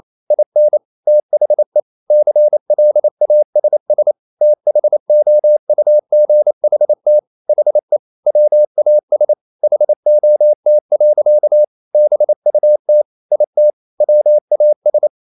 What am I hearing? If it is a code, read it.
Morse code, heard as a keyed tone, about 28 wpm, 600 Hz. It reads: IN THE CLASS THOUGHT HE WAS HOT. BUT IT WAS
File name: Never 0027wpm